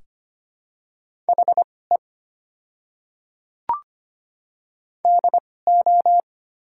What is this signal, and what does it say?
Morse code, heard as a keyed tone, about 25 wpm, spaced as for 18 wpm, 700 Hz. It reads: HE DO
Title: 2x Character Spacing - Top 100 Words - 25wpm